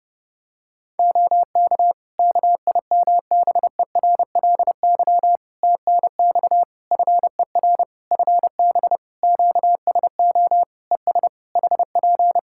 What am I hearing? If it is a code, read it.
Morse code, heard as a keyed tone, about 30 wpm, 700 Hz. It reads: OK KIMBERLY TNX FER FB QSO ES HP
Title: Practice QSO - Exchange 4 - Ending - 30wpm